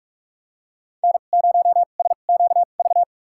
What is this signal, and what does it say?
Morse code, heard as a keyed tone, about 46 wpm, spaced as for 33 wpm, 700 Hz. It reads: N0SQV
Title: USA Call Signs - 5 Characters - 2x Character Spacing - 45wpm